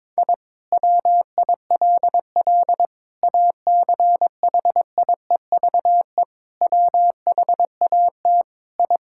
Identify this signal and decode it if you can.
Morse code, heard as a keyed tone, about 22 wpm, 700 Hz. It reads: I WILL ACHIEVE WHAT I
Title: Sentences from Top 1000 Words - Review Part 6 - 22wpm